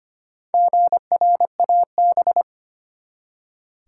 Morse code, heard as a keyed tone, about 25 wpm, 700 Hz. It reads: GRAB
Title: Four Letter Words - 25wpm